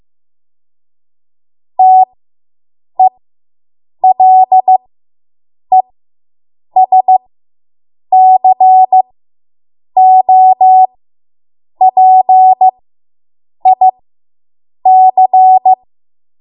Morse code, heard as a keyed tone, about 15 wpm, spaced as for 7 wpm, 750 Hz. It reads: TELESCOPIC